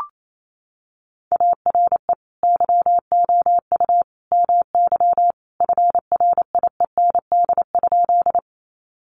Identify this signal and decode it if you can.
Morse code, heard as a keyed tone, about 28 wpm, 700 Hz. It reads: ARE YOU MY FRIEND?
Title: Sentences from Top 300 Words 28wpm